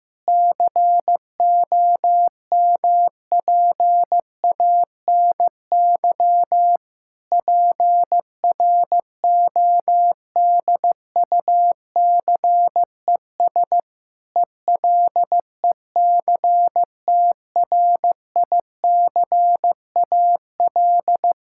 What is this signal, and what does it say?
Morse code, heard as a keyed tone, about 15 wpm, 700 Hz. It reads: COMPANY PRODUCES ELECTRICAL
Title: Sentences from Top 1000 Words - Review Part 6 - 15wpm